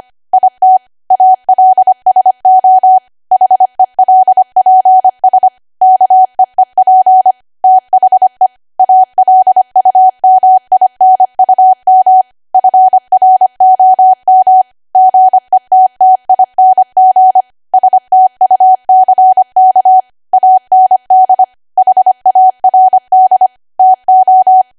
Morse code, heard as a keyed tone, about 25 wpm, 750 Hz. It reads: IT ALSO HELPS KEEP THE ALUMINUM FROM GETTING STUCK AND HARD TO
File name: Archive / 25 WPM / 25WPM